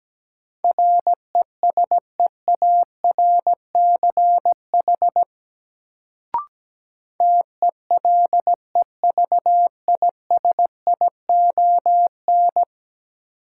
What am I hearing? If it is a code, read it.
Morse code, heard as a keyed tone, about 17 wpm, 700 Hz. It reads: RESEARCH TELEVISION
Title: Sentences from Top 1000 Words - Review Part 4 - New Word Drill - 17wpm